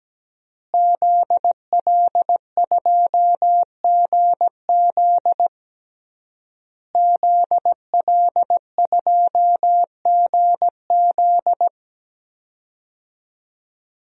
Morse code, heard as a keyed tone, about 17 wpm, 700 Hz. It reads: ZL2GZ ZL2GZ
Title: DXCC-100 Call Signs - 5 Characters - Repeated 3x - 17wpm